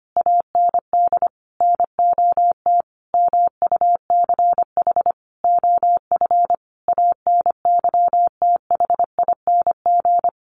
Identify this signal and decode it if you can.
Morse code, heard as a keyed tone, about 25 wpm, 700 Hz. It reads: AND NOT MUCH OF ANYTHING
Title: Sentences from Top 1500 Words 25wpm